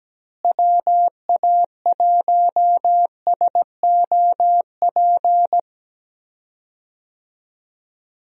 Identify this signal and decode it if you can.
Morse code, heard as a keyed tone, about 17 wpm, 700 Hz. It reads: WA1SOP
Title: USA Call Signs - 6 Characters - Repeated 2x - 17wpm